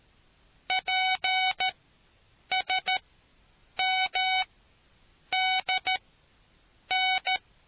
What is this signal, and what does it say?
Morse code, heard as a keyed tone, about 13 wpm, spaced as for 7 wpm, 750 Hz. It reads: PSMDN